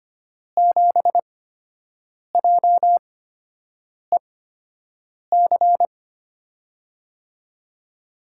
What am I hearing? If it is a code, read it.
Morse code, heard as a keyed tone, about 25 wpm, spaced as for 7 wpm, 700 Hz. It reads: 7JEC
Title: USA Call Signs - 5 Characters - 8x Character Spacing - 25wpm